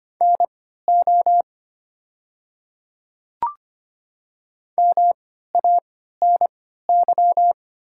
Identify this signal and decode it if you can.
Morse code, heard as a keyed tone, about 25 wpm, spaced as for 14 wpm, 700 Hz. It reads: NO MANY
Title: 3x Character Spacing - Top 100 Words - 25wpm